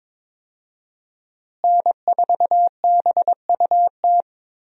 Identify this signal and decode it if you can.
Morse code, heard as a keyed tone, about 22 wpm, 700 Hz. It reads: N4BUT
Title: USA Call Signs - 5 Characters - Repeated 3x - 22wpm